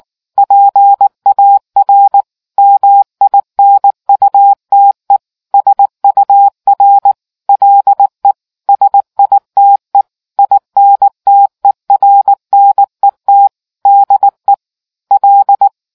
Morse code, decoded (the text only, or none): PAR MINUTE SUR LE SITE INTERNET DE L